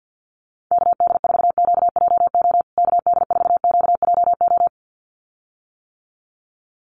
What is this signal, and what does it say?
Morse code, heard as a keyed tone, about 50 wpm, 700 Hz. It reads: KD4QJO KD4QJO
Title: USA Call Signs Repeated 2x 50wpm